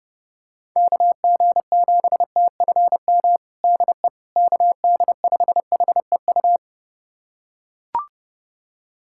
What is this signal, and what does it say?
Morse code, heard as a keyed tone, about 32 wpm, 700 Hz. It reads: KG7TFM DE KD5HEU
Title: USA Callsign DE Callsign - 6 Characters - 30wpm